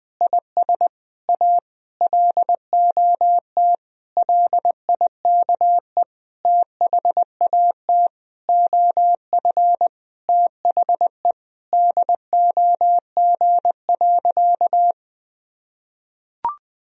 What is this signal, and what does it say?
Morse code, heard as a keyed tone, about 20 wpm, 700 Hz. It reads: IS A LOT LIKE THAT OF THE DOG.
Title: Sentences from Top 1000 Words - Review Part 2 - 20wpm